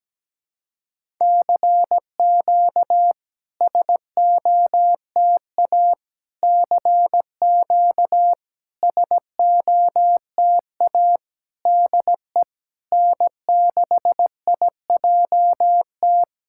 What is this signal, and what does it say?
Morse code, heard as a keyed tone, about 17 wpm, 700 Hz. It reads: CQ SOTA CQ SOTA DE N6IJT
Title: SOTA QSO Practice 17wpm